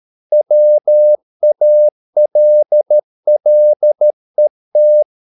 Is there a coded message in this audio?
Yes: Morse code from 0.3 to 5.0 s, heard as a keyed tone, about 13 wpm, 600 Hz. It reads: WALLET